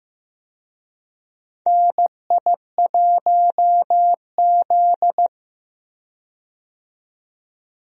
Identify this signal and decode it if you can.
Morse code, heard as a keyed tone, about 15 wpm, 700 Hz. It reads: NI1Z
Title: USA Call Signs - 4 Characters - 15wpm